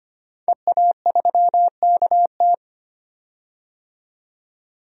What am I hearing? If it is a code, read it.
Morse code, heard as a keyed tone, about 25 wpm, 700 Hz. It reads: EA3KT
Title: DXCC-100 Call Signs - 5 Characters - 25wpm